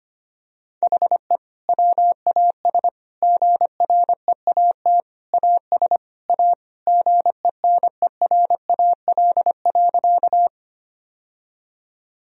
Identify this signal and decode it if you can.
Morse code, heard as a keyed tone, about 25 wpm, 700 Hz. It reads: HE WAS GREAT AS A GENERAL.
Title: Sentences from Top 1000 Words - Review Part 3 - 25wpm